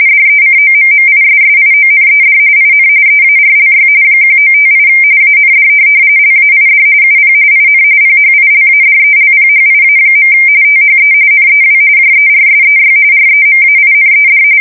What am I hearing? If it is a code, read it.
RTTY, about 45.45 baud. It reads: DECOMMISSIONED IN EARLY 1958 AND LAID UP IN PHILADELPHIA. THE NEW JERSEY REMAINED
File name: fsk170 / USS-New-Jersey